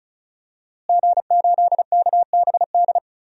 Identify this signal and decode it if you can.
Morse code, heard as a keyed tone, about 34 wpm, 700 Hz. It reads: G8KBD
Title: DXCC-100 Call Signs - 5 Characters - Repeated 2x - 35wpm